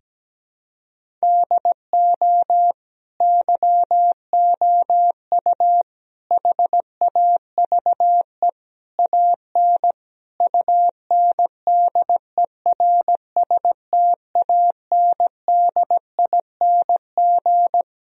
Morse code, heard as a keyed tone, about 17 wpm, 700 Hz. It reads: DO YOU HAVE AN UNDERSTANDING
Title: Sentences from Top 1000 Words - Review Part 3 - 17wpm